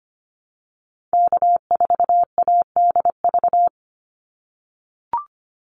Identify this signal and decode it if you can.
Morse code, heard as a keyed tone, about 25 wpm, 700 Hz. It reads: K4ADV
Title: NA SOTA Activator Callsigns 25wpm